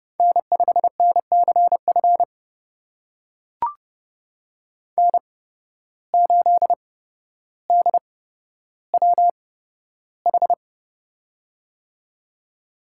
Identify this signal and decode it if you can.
Morse code, heard as a keyed tone, about 30 wpm, spaced as for 13 wpm, 700 Hz. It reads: N5NCF N8DWH
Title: USA Call Signs - 5 Characters - 8x Character Spacing - 30wpm